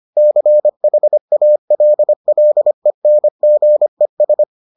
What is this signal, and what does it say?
Morse code, heard as a keyed tone, about 25 wpm, 600 Hz. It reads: CHALLENGES